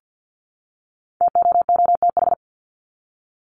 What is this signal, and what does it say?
Morse code, heard as a keyed tone, about 50 wpm, 700 Hz. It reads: TOOTH
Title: Top 1751-2000 Words 50wpm